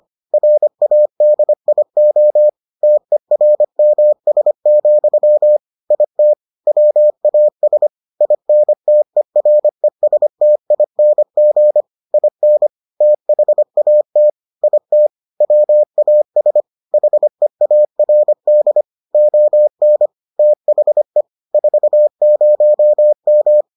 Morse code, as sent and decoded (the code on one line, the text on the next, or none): .-. .- -.. .. --- / - . .-. -- ... --..-- / .. - / .-- .- ... / .. -. - . .-. ... - .. -. --. / .. -. / - .... .- - / .. - / .-- .- ... / .... .- .-. -.. / --- -. / - .... / ....- ----- --
RADIO TERMS, IT WAS INTERESTING IN THAT IT WAS HEARD ON THE 40M